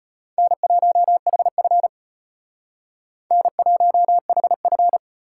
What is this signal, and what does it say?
Morse code, heard as a keyed tone, about 33 wpm, 700 Hz. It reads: N1HF N1HF